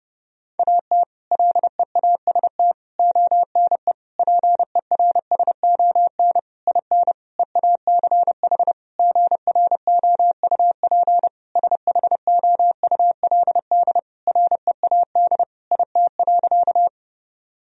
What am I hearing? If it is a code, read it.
Morse code, heard as a keyed tone, about 30 wpm, 700 Hz. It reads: AT LEAST ONE PERSON IN EACH GROUP SHOULD READ IT.
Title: Sentences from Top 1000 Words - Review Part 1 - 30wpm